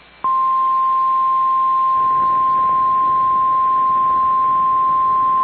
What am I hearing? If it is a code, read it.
Emergency Alert System (SAME) tones (the 1050 Hz attention tone). It is heard from 0.4 to 4.4 s.
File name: EAS Logs / KMNR